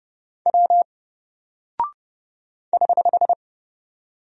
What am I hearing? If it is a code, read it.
Morse code, heard as a keyed tone, about 30 wpm, 700 Hz. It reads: W <HH>
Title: Single Character (Letter:Number:Punctuation:Prosign) Alt 30wpm